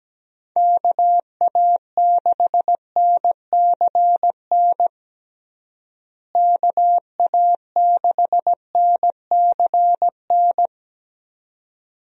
Morse code, heard as a keyed tone, about 17 wpm, 700 Hz. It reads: KA6NCN KA6NCN
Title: USA Call Signs - 6 Characters - Repeated 2x - 17wpm